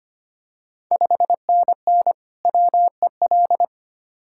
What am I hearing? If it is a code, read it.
Morse code, heard as a keyed tone, about 25 wpm, 700 Hz. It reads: 5NN WEL
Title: Ontario QSO Party Exchange 25wpm